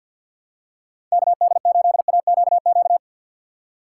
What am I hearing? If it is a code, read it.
Morse code, heard as a keyed tone, about 52 wpm, 700 Hz. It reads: KD8AQQ
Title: USA Call Signs - 6 Characters - 6x Character Spacing - 50wpm